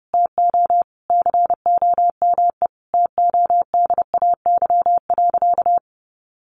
Morse code, heard as a keyed tone, about 30 wpm, 700 Hz. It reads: TO COME TODAY.
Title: Easy Phrases 30wpm